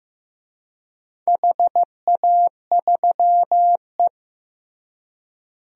Morse code, heard as a keyed tone, about 15 wpm, 700 Hz. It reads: HA3E
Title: DXCC-100 Call Signs - 4 Characters - Repeated 2x - 15wpm